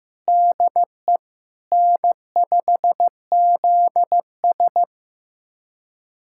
Morse code, heard as a keyed tone, about 15 wpm, 700 Hz. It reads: DE N5ZS
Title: USA Callsign DE Callsign - 4 Characters - 15wpm